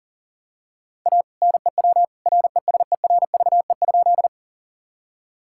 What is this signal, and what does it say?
Morse code, heard as a keyed tone, about 37 wpm, 700 Hz. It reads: A NEW RESERVE?
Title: Sentences from Top 1000 Words - Review Part 6 - 40wpm